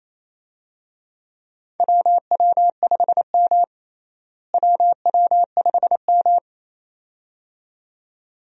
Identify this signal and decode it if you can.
Morse code, heard as a keyed tone, about 28 wpm, 700 Hz. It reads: WW5M WW5M
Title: USA Call Signs - 4 Characters - Repeated 2x - 28wpm